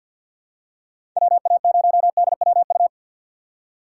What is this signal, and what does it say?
Morse code, heard as a keyed tone, about 51 wpm, 700 Hz. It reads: WA0DWU
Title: USA Call Signs - 6 Characters - 4x Character Spacing - 50wpm